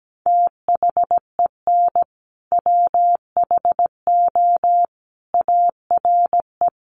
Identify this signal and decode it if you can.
Morse code, heard as a keyed tone, about 17 wpm, 700 Hz. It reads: THEN WHO ARE
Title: Sentences from Top 100 Words 17wpm